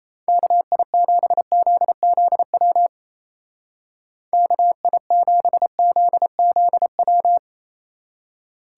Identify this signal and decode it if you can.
Morse code, heard as a keyed tone, about 35 wpm, 700 Hz. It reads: KI7ZZW KI7ZZW
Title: Speed Racing - USA Call Signs 28wpm